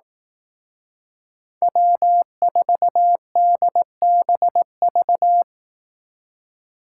Morse code, heard as a keyed tone, about 18 wpm, 700 Hz. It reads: W4DBV
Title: Speed Racing - USA 1x3 Call Signs 15wpm